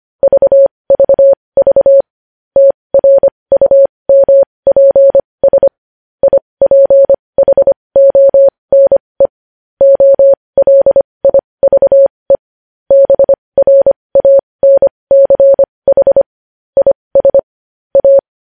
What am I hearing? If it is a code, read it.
Morse code, heard as a keyed tone, about 25 wpm, spaced as for 20 wpm, 550 Hz. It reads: VVV TRUMPS IPHONE OLIVE BRANCH IS A
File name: podcast / News headlines-1x-25x20